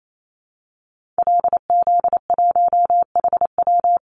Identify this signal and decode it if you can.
Morse code, heard as a keyed tone, about 28 wpm, 700 Hz. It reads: LZ1HW
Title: Call Signs Repeated 3x - 28wpm